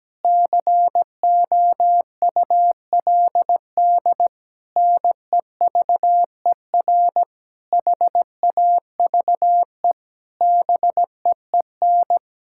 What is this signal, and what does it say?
Morse code, heard as a keyed tone, about 17 wpm, 700 Hz. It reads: COULD NEVER HAVE BEEN
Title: Sentences from Top 1000 Words - Review Part 6 - 17wpm